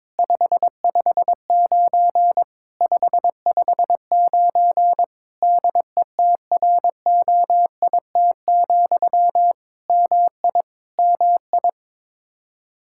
Morse code, heard as a keyed tone, about 22 wpm, 700 Hz. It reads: 559 559 DETROIT, MI MI
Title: Practice QSO - Exchange 1 - Beginning - 22wpm